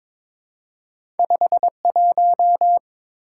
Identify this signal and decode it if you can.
Morse code, heard as a keyed tone, about 22 wpm, 700 Hz. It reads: 51
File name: practice / 22wpm / Two Letter-Number Combinations - 22wpm